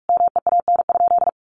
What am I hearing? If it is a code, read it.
Morse code, heard as a keyed tone, about 44 wpm, 700 Hz. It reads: MEAN?
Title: Sentences from Top 200 Words 45wpm